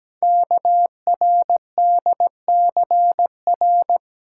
Morse code, heard as a keyed tone, about 17 wpm, 700 Hz. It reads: KRDCR
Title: Code Groups - 5 Letters - 17wpm